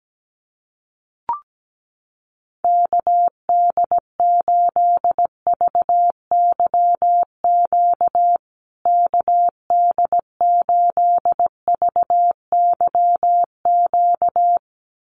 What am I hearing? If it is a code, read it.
Morse code, heard as a keyed tone, about 17 wpm, 700 Hz. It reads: KD8VYQ KD8VYQ
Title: USA Call Signs Repeated 2x 17wpm